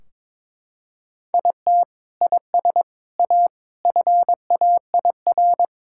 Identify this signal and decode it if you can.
Morse code, heard as a keyed tone, about 22 wpm, 700 Hz. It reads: IT IS A FAIR
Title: Sentences from Top 1600 Words 22wpm